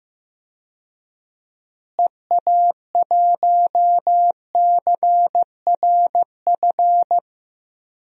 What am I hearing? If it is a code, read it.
Morse code, heard as a keyed tone, about 15 wpm, 700 Hz. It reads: EA1CRF
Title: DXCC-100 Call Signs - 6 Characters - Repeated 2x - 15wpm